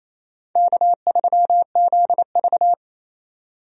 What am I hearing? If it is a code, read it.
Morse code, heard as a keyed tone, about 28 wpm, 700 Hz. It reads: K3ZV
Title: USA Call Signs - 4 Characters - 28wpm